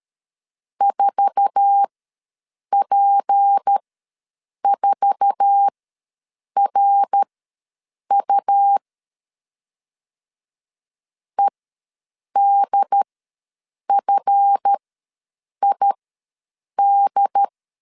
Morse code, heard as a keyed tone, about 13 wpm, spaced as for 7 wpm, 800 Hz. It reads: 4P4RU EDFID
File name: LEZIONI / 8 YJ45 / 8 YJ4 ws 12wpm 3249